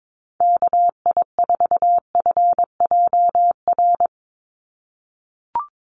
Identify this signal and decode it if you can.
Morse code, heard as a keyed tone, about 22 wpm, 700 Hz. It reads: KI4FJR
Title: USA Call Signs 22wpm